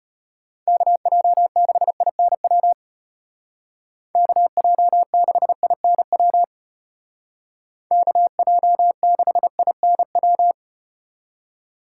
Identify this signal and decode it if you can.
Morse code, heard as a keyed tone, about 35 wpm, 700 Hz. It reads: KJ6INW KJ6INW KJ6INW